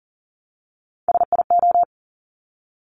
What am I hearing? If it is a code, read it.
Morse code, heard as a keyed tone, about 39 wpm, 690 Hz. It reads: SIO
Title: Top Three Letter Combinations in Words 40wpm